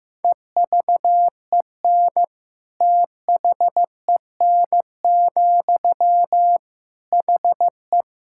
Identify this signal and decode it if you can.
Morse code, heard as a keyed tone, about 15 wpm, 700 Hz. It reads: EVEN THEN, HE
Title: Sentences from Top 1600 Words 15wpm